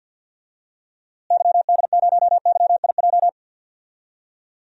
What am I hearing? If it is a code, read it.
Morse code, heard as a keyed tone, about 52 wpm, 700 Hz. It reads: YD0YIJ
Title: DXCC-100 Call Signs - 6 Characters - Repeated 3x - 50wpm